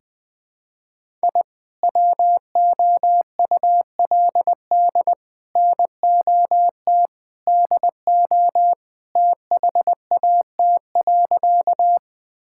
Morse code, heard as a keyed tone, about 21 wpm, 700 Hz. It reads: I WOULD NOT DO THAT.
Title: Sentences with 3-Word Phrases - 20wpm